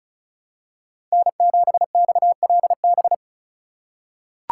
Morse code, heard as a keyed tone, about 36 wpm, 700 Hz. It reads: N7XLB
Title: USA Call Signs - 5 Characters - 2x Character Spacing - 35wpm